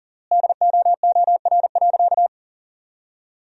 Morse code, heard as a keyed tone, about 40 wpm, 700 Hz. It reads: DOOR.
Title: 3x Word-Spacing - Sentences from Top 400 Words - 40wpm